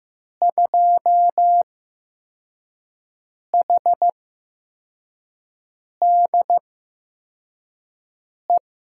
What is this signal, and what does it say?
Morse code, heard as a keyed tone, about 15 wpm, spaced as for 4 wpm, 700 Hz. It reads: 2HDE
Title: USA Call Signs - 6 Characters - 8x Character Spacing - 15wpm